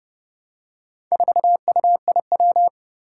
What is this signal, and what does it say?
Morse code, heard as a keyed tone, about 30 wpm, 700 Hz. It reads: 4UIW
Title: Four Challenging Letter-Number Combinations 30wpm